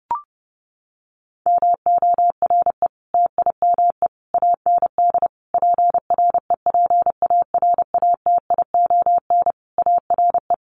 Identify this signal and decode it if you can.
Morse code, heard as a keyed tone, about 30 wpm, 700 Hz. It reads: MORE TIME AND PREPARATION ARE
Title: Sentences from Top 1400 Words 30wpm